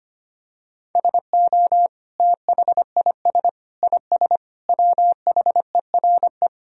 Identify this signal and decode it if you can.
Morse code, heard as a keyed tone, about 25 wpm, 700 Hz. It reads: SO THIS IS WHERE
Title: Sentences from Top 1000 Words - Review Part 2 - 25wpm